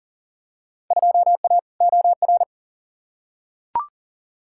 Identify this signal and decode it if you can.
Morse code, heard as a keyed tone, about 40 wpm, 700 Hz. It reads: 2A OR
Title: Field Day Exchanges - 40wpm